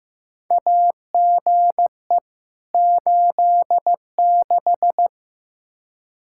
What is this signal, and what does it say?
Morse code, heard as a keyed tone, about 15 wpm, 700 Hz. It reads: AGE 86
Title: K1USN SST QSO Practice 15wpm